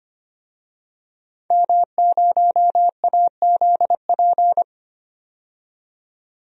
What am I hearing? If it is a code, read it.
Morse code, heard as a keyed tone, about 25 wpm, 700 Hz. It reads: M0AZP
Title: DXCC-100 Call Signs - 5 Characters - 25wpm